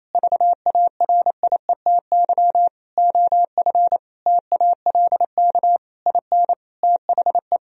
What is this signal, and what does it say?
Morse code, heard as a keyed tone, about 28 wpm, 700 Hz. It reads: VARIETY OF TALK IN THE
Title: Sentences from Top 1000 Words - Review Part 4 - 28wpm